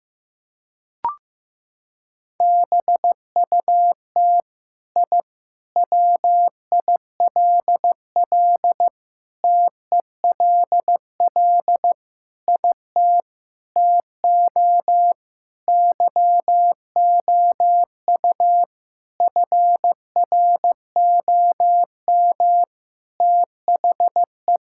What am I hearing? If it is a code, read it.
Morse code, heard as a keyed tone, about 15 wpm, 700 Hz. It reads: BUT I WILL TELL IT TO YOU FROM THE
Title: Sentences from Top 1000 Words - Review Part 5 - 15wpm